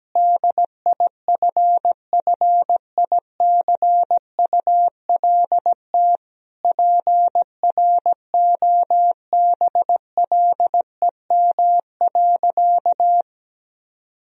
Morse code, heard as a keyed tone, about 17 wpm, 700 Hz. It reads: DIFFICULT PROBLEM.
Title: Sentences from Top 1000 Words - Review Part 2 - 17wpm